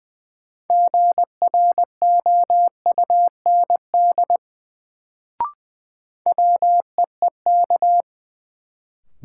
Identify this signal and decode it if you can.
Morse code, heard as a keyed tone, about 20 wpm, 700 Hz. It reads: GROUND WEEK
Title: Sentences from Top 500 Words - New Word Drill - 20wpm